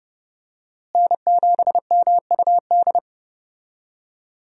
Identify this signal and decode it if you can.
Morse code, heard as a keyed tone, about 29 wpm, 700 Hz. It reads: N7MUD
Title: USA Call Signs - 5 Characters - Repeated 2x - 30wpm